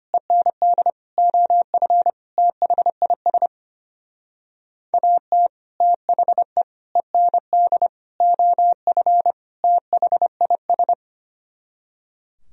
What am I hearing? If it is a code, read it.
Morse code, heard as a keyed tone, about 33 wpm, 700 Hz. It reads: END OF THIS AT THE END OF THIS